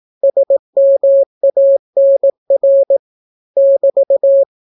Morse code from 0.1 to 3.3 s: SMANR